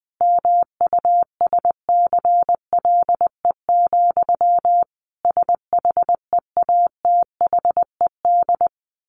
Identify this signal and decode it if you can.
Morse code, heard as a keyed tone, about 20 wpm, 700 Hz. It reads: MUSCLE, SHEATHED
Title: A Sound Of Thunder 20wpm